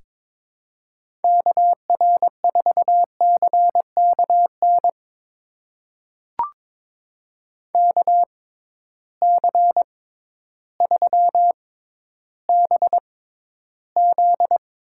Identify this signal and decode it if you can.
Morse code, heard as a keyed tone, about 22 wpm, spaced as for 12 wpm, 700 Hz. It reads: KR4CKN KC3BZ
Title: USA Call Signs - 6 Characters - 6x Character Spacing - 22wpm